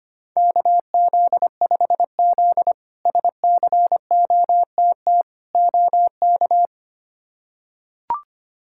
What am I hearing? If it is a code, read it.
Morse code, heard as a keyed tone, about 25 wpm, 700 Hz. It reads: KZ5Z SCOTT OK
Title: K1USN SST QSO Practice - 25wpm